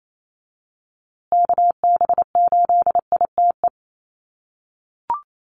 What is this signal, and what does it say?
Morse code, heard as a keyed tone, about 28 wpm, 700 Hz. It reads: KB8ITE
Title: USA Call Signs Repeated 2x 28wpm